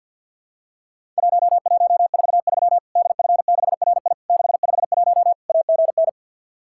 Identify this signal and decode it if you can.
Morse code, heard as a keyed tone, about 50 wpm, 700 Hz. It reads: 1143 DF6RI 651 AGN
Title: Worked All Europe DX Contest - QTC Exchange with Repeats - 50wpm